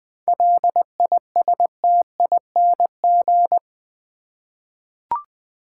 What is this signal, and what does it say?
Morse code, heard as a keyed tone, about 20 wpm, 700 Hz. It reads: LISTING
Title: 7 Letter Words - First 3 Letters Given - 20wpm